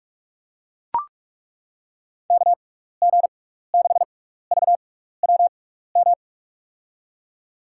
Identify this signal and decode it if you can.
Morse code, heard as a keyed tone, about 41 wpm, spaced as for 15 wpm, 700 Hz. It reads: KG6VWM
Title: USA Call Signs - 6 Characters - 6x Character Spacing - 45wpm